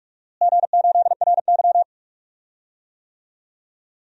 Morse code, heard as a keyed tone, about 46 wpm, 700 Hz. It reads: G8RY
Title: DXCC-100 Call Signs - 4 Characters - Repeated 2x - 45wpm